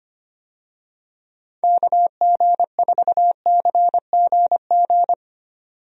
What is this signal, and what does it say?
Morse code, heard as a keyed tone, about 25 wpm, 700 Hz. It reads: KG4CGG
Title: USA Call Signs - 6 Characters - 25wpm